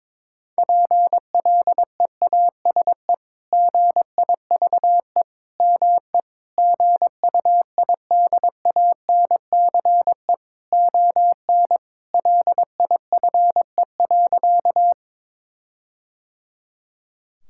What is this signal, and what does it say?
Morse code, heard as a keyed tone, about 22 wpm, 700 Hz. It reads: PLEASE GIVE ME GUIDANCE ON LIFE.
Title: Sentences from Top 1600 Words 22wpm